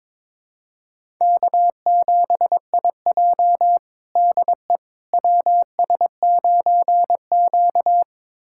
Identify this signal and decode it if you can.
Morse code, heard as a keyed tone, about 22 wpm, 700 Hz. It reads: K7IJ DE WS9Q
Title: USA Callsign DE Callsign - 4 Characters - 22wpm